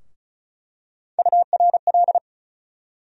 Morse code, heard as a keyed tone, about 36 wpm, 700 Hz. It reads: URL
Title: USA Call Sign Postfixes - 3 Letters 35wpm